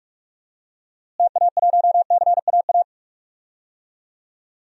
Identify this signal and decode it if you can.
Morse code, heard as a keyed tone, about 41 wpm, 700 Hz. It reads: TA1CAA
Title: DXCC-100 Call Signs - 6 Characters - Repeated 2x - 45wpm